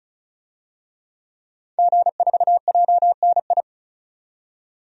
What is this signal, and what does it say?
Morse code, heard as a keyed tone, about 36 wpm, 700 Hz. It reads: G4JNI
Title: DXCC-100 Call Signs - 5 Characters - 35wpm